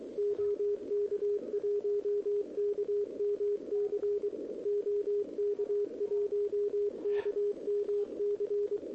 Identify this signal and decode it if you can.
Morse code, heard as a keyed tone, about 22 wpm, 410 Hz, with radio noise. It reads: OK1KMC OK1KMC
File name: OK QRP 2026 / OKQRP2026